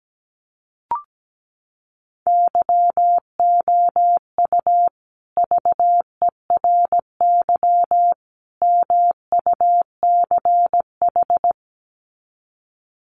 Morse code, heard as a keyed tone, about 17 wpm, 700 Hz. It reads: YOU VERY MUCH
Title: Sets of 3 Words 17wpm